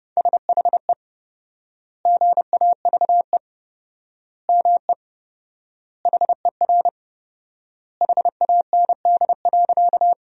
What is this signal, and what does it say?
Morse code, heard as a keyed tone, about 29 wpm, 700 Hz. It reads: SHE GAVE ME HER HAND.
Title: 4x Word-Spacing - Sentences from Top 500 Words - 30wpm